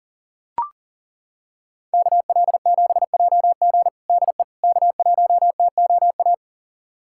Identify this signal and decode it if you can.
Morse code, heard as a keyed tone, about 37 wpm, 700 Hz. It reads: KL7JG DE K1TOA
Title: USA Callsign DE Callsign - 5 Characters - 40wpm